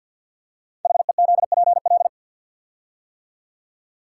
Morse code, heard as a keyed tone, about 48 wpm, 700 Hz. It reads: HEZPL
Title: Code Groups - 5 Letters - 50wpm